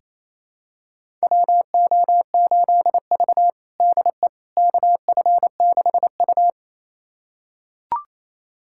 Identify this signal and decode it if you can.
Morse code, heard as a keyed tone, about 28 wpm, 700 Hz. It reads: WO8V DE KF6U
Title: USA Callsign DE Callsign - 4 Characters - 28wpm